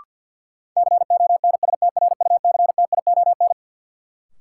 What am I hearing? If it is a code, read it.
Morse code, heard as a keyed tone, about 53 wpm, 700 Hz. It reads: CONSTRUCTION
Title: Sentences from Top 1600 Words - New Word Drill Alt - 50wpm